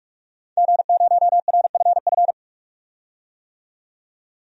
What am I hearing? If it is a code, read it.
Morse code, heard as a keyed tone, about 46 wpm, 700 Hz. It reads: G0RFF
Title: DXCC-100 Call Signs - 5 Characters - 45wpm